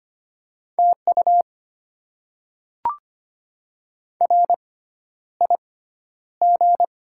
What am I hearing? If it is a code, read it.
Morse code, heard as a keyed tone, about 25 wpm, spaced as for 11 wpm, 700 Hz. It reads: TU RIG
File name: Top 20 QSO Elements - 6x Character Spacing - 25wpm